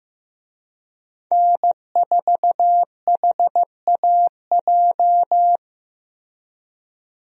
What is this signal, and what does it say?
Morse code, heard as a keyed tone, about 15 wpm, 700 Hz. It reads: N4HAJ
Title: USA Call Signs - 5 Characters - Repeated 3x - 15wpm